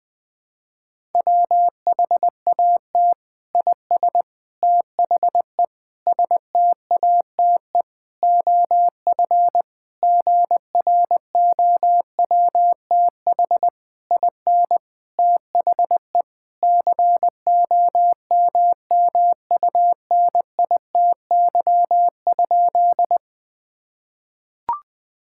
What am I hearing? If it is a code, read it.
Morse code, heard as a keyed tone, about 20 wpm, 700 Hz. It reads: WHAT IS THE STATE OF GROWTH IN THE COMMUNITY?
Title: Sentences from Top 1000 Words - Review Part 5 - 20wpm